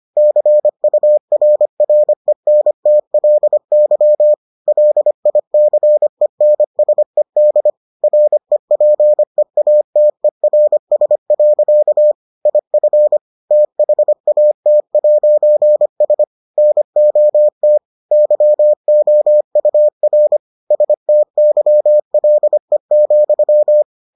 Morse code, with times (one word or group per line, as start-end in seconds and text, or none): 0.2-4.3 CURRENTLY
4.7-7.7 LICENSED
8.0-12.1 REPEATERS.
12.5-13.2 IF
13.5-16.2 THAT'S
16.6-17.8 NOT
18.1-20.4 YOUR
20.7-23.8 STYLE,